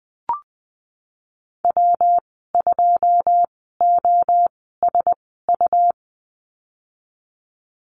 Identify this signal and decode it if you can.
Morse code, heard as a keyed tone, about 20 wpm, spaced as for 15 wpm, 700 Hz. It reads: W2OSU
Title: 186 Lesson 30g - Letter b - Call Signs 20wpm